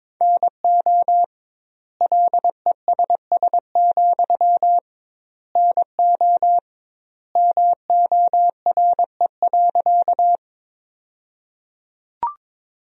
Morse code, heard as a keyed tone, about 22 wpm, 700 Hz. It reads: NO LESS, NO MORE.
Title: 2x Word-Spacing - Sentences from Top 400 Words - 22wpm